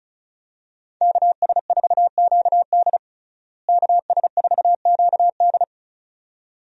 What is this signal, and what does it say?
Morse code, heard as a keyed tone, about 34 wpm, 700 Hz. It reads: KS4QD KS4QD
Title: USA Call Signs - 5 Characters - Repeated 2x - 35wpm